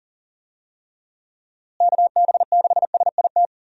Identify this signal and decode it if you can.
Morse code, heard as a keyed tone, about 37 wpm, 700 Hz. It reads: KB6SIT
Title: USA Call Signs - 6 Characters - Repeated 3x - 40wpm